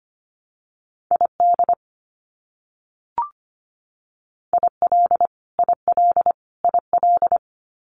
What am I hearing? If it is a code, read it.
Morse code, heard as a keyed tone, about 25 wpm, 700 Hz. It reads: ID IL IL IL
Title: US States Repeated 3x 25wpm